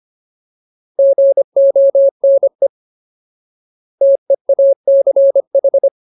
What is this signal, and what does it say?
Morse code, heard as a keyed tone, about 25 wpm, 550 Hz. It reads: GONE TEACH
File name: Ultimate Word List / 25WPM Ultimate